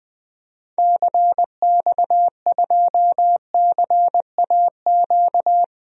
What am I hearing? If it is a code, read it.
Morse code, heard as a keyed tone, about 20 wpm, 700 Hz. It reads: CX2CAQ
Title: DXCC-100 Call Signs - 6 Characters - 20wpm